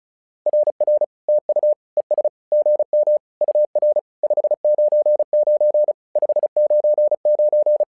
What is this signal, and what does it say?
Morse code, heard as a keyed tone, about 36 wpm, 600 Hz. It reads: RR TU ES GM UR 599 599
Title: SOTA QSO Practice 35wpm